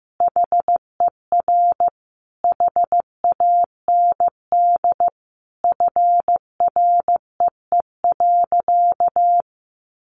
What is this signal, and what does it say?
Morse code, heard as a keyed tone, about 15 wpm, 700 Hz. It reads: HER HAND FREE.
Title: Sentences from Top 500 Words 15wpm